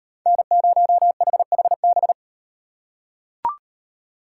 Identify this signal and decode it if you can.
Morse code, heard as a keyed tone, about 38 wpm, 700 Hz. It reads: N0HHB